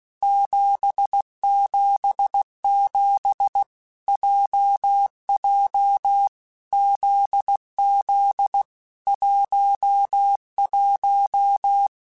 Morse code, heard as a keyed tone, about 16 wpm, 800 Hz. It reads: 777 JJ ZZ 11
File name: Basic Letter Groups By7Groups / JZ17 / JZ17@15